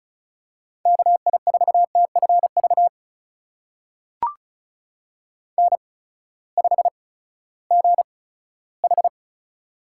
Morse code, heard as a keyed tone, about 36 wpm, spaced as for 18 wpm, 700 Hz. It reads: KI4TFV N5GH
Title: USA Call Signs - 8x Character Spacing - 35wpm